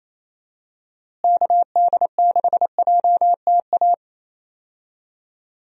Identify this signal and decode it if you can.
Morse code, heard as a keyed tone, about 28 wpm, 700 Hz. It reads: KD6JTA
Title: USA Call Signs - 6 Characters - Repeated 3x - 28wpm